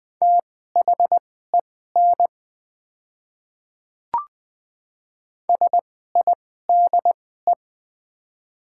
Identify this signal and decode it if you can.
Morse code, heard as a keyed tone, about 20 wpm, spaced as for 15 wpm, 690 Hz. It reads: THEN SIDE
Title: 2x Character Spacing - Top 100 Words - 20wpm